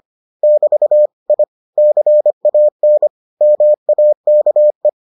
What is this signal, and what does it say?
Morse code, heard as a keyed tone, about 25 wpm, 600 Hz. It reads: <BT> I CAN MAKE